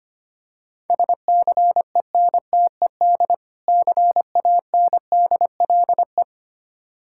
Morse code, heard as a keyed tone, about 25 wpm, 690 Hz. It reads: SCENTED CANDLE
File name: practice / 25wpm / Difficult Words - Speed-Racing with Hints - 25wpm